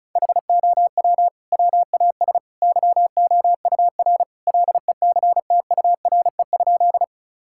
Morse code, heard as a keyed tone, about 36 wpm, 700 Hz. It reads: HOW WAS YOUR LECTURE?
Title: Sentences from Top 1600 Words 35wpm